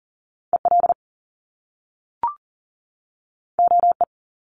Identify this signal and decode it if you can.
Morse code, heard as a keyed tone, about 39 wpm, 700 Hz. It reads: EL OE
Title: Top Two Letter Combinations in Words 40wpm